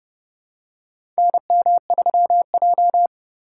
Morse code, heard as a keyed tone, about 30 wpm, 700 Hz. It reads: NM3J
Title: USA Call Signs - 4 Characters - 4x Character Spacing - 30wpm